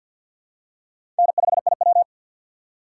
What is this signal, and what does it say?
Morse code, heard as a keyed tone, about 52 wpm, 700 Hz. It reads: N5IW
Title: USA Call Signs - 3x Character Spacing - 50wpm